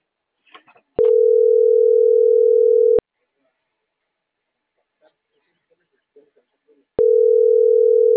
Ringback tone (North America), 440 + 480 Hz.